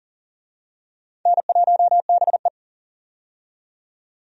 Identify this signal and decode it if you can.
Morse code, heard as a keyed tone, about 40 wpm, 700 Hz. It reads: N1BE
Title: USA Call Signs - 4 Characters - 40wpm